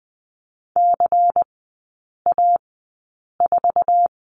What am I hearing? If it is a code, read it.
Morse code, heard as a keyed tone, about 20 wpm, spaced as for 8 wpm, 700 Hz. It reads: CA4
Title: 093 Lesson 16e - Letter c - Three Characters 20wpm